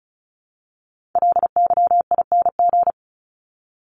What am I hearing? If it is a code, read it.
Morse code, heard as a keyed tone, about 34 wpm, 700 Hz. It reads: LYING
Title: ING Suffix - Top 100 Words with Base Words 35wpm